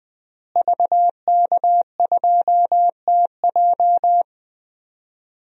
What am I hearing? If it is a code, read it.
Morse code, heard as a keyed tone, about 20 wpm, 700 Hz. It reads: VK2TJ
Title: DXCC-100 Call Signs - 5 Characters - Repeated 2x - 20wpm